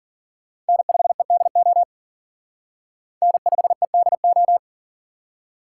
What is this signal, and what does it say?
Morse code, heard as a keyed tone, about 52 wpm, 700 Hz. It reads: N5EDO N5EDO